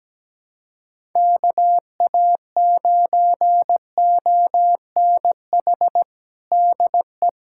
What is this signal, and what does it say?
Morse code, heard as a keyed tone, about 17 wpm, 700 Hz. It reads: KA9ONH DE
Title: USA Callsign DE Callsign - 6 Characters - 17wpm